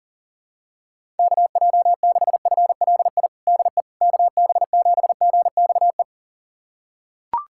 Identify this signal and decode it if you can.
Morse code, heard as a keyed tone, about 40 wpm, 700 Hz. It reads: KJ6FLI DE KB7GXE
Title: USA Callsign DE Callsign - 6 Characters - 40wpm